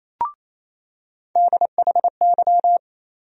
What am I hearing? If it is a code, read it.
Morse code, heard as a keyed tone, about 28 wpm, 700 Hz. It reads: DHY
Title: USA Call Sign Postfixes - 3 Letters 28wpm